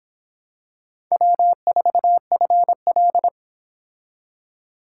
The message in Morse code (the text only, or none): W4FL